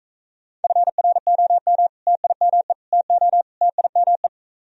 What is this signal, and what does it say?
Morse code, heard as a keyed tone, about 45 wpm, 700 Hz. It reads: FROM TIME TO TIME